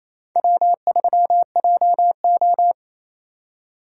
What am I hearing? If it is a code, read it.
Morse code, heard as a keyed tone, about 28 wpm, 700 Hz. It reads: W3JO
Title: USA Call Signs - 4 Characters - 2x Character Spacing - 28wpm